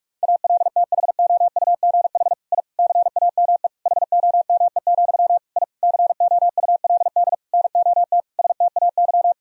Morse code, heard as a keyed tone, about 41 wpm, 700 Hz. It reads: ALTHOUGH I CAME HOME, I COULD NOT STAY
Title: Sentences from Top 1000 Words - Review Part 3 - 45wpm